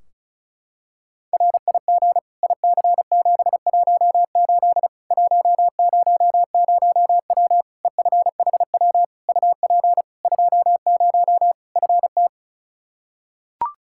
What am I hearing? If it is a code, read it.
Morse code, heard as a keyed tone, about 34 wpm, 700 Hz. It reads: RIG IC718 100W EFHW UP 20 FT
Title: Practice QSO - Exchange 2 - Info - 35wpm